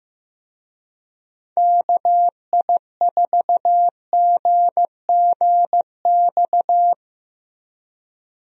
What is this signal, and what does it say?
Morse code, heard as a keyed tone, about 15 wpm, 700 Hz. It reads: KI4GGX
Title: USA Call Signs - 6 Characters - Repeated 3x - 15wpm